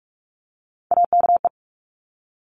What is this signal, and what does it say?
Morse code, heard as a keyed tone, about 46 wpm, 700 Hz. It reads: AKE
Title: Top Three Letter Combinations in Words 45wpm